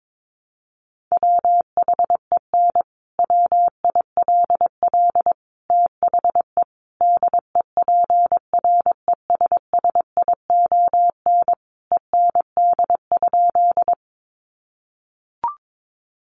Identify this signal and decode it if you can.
Morse code, heard as a keyed tone, about 22 wpm, 700 Hz. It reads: WHEN WILL THE DEPRESSION END?
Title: Sentences from Top 1300 Words 22wpm